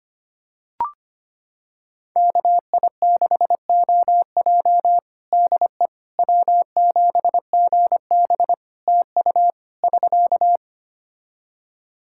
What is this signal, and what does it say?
Morse code, heard as a keyed tone, about 25 wpm, 700 Hz. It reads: KI6OJ DE W7GB TU <SK>
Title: Practice QSO - Exchange 4 - Ending - 25wpm